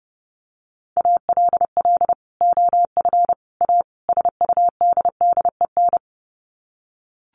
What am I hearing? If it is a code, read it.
Morse code, heard as a keyed tone, about 30 wpm, 700 Hz. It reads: ALL OF A SUDDEN
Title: Sets of 4 Words 30wpm